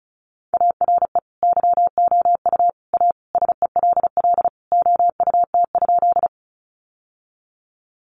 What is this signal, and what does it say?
Morse code, heard as a keyed tone, about 34 wpm, 700 Hz. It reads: ARE YOU A SELL OUT?
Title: Sentences from Top 1100 Words 35wpm